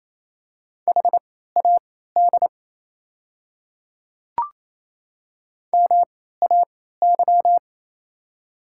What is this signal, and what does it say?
Morse code, heard as a keyed tone, about 27 wpm, spaced as for 16 wpm, 700 Hz. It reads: HAD MAY
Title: 3x Character Spacing - Top 100 Words - 28wpm